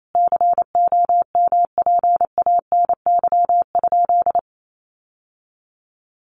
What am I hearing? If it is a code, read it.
Morse code, heard as a keyed tone, about 28 wpm, 700 Hz. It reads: COMPANY?
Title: Sentences from Top 600 Words 28wpm